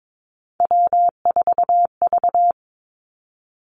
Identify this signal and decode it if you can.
Morse code, heard as a keyed tone, about 22 wpm, 700 Hz. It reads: W4V
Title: Three Challenging Letter-Number Combinations 22wpm